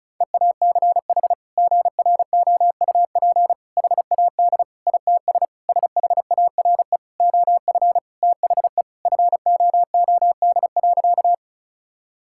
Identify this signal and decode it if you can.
Morse code, heard as a keyed tone, about 36 wpm, 700 Hz. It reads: EACH GROUP HAD ITS SHARE OF THE FOOD.
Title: Sentences from Top 1000 Words - Review Part 6 - 35wpm